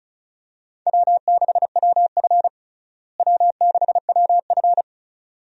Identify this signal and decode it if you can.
Morse code, heard as a keyed tone, about 36 wpm, 700 Hz. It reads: W6WF W6WF
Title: USA Call Signs - 4 Characters - Repeated 2x - 35wpm